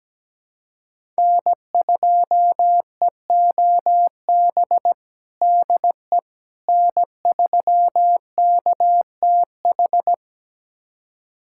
Morse code, heard as a keyed tone, about 17 wpm, 700 Hz. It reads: N2EOB DE N3KTH
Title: USA Callsign DE Callsign - 5 Characters - 17wpm